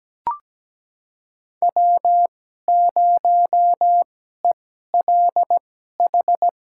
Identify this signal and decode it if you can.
Morse code, heard as a keyed tone, about 17 wpm, spaced as for 12 wpm, 700 Hz. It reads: W0ELH
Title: USA Call Signs - 2x Character Spacing - 17wpm